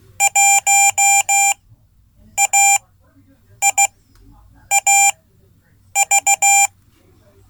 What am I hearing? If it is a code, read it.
Morse code, heard as a keyed tone, about 16 wpm, spaced as for 8 wpm, 790 Hz. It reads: 1AIAV
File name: Code / 5min-c